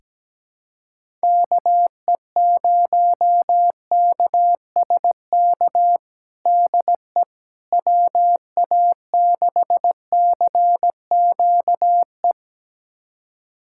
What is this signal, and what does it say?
Morse code, heard as a keyed tone, about 17 wpm, 700 Hz. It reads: KE0KSK DE WA6CQE
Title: USA Callsign DE Callsign 17wpm